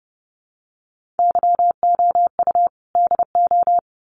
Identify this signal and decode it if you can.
Morse code, heard as a keyed tone, about 30 wpm, 700 Hz. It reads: YOU DO
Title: Sets of 2 Words 30wpm